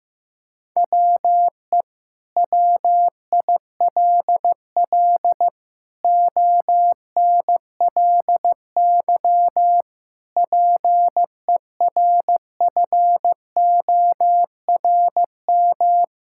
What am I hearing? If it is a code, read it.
Morse code, heard as a keyed tone, about 15 wpm, 700 Hz. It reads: WE WILL ONLY PERFORM
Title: Sentences from Top 1000 Words - Review Part 5 - 15wpm